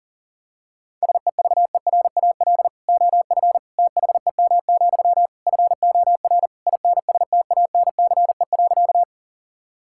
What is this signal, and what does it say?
Morse code, heard as a keyed tone, about 40 wpm, 700 Hz. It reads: SEVERAL OF THEM, FOR INSTANCE.
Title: Sentences from Top 1000 Words - Review Part 5 - 40wpm